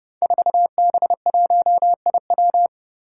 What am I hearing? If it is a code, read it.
Morse code, heard as a keyed tone, about 30 wpm, 700 Hz. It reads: 4B1IW
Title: Five Challenging Letter-Number Combinations 30wpm